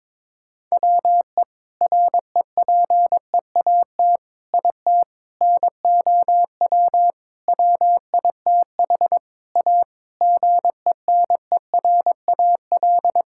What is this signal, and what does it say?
Morse code, heard as a keyed tone, about 22 wpm, 700 Hz. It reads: WE REPEAT IT NOW WITH A GENERAL
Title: Sentences from Top 1600 Words 22wpm